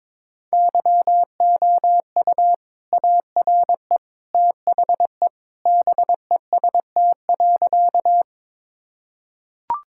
Morse code, heard as a keyed tone, about 22 wpm, 700 Hz. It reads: YOU ARE THE BEST.
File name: practice / 22wpm / Sentences with 2-Word Phrases - 22wpm